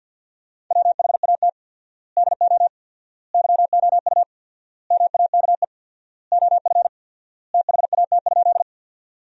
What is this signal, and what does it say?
Morse code, heard as a keyed tone, about 48 wpm, 700 Hz. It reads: WHAT DO YOU MAKE OF THAT?
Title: 4x Word-Spacing - Sentences from Top 100 Words - 50wpm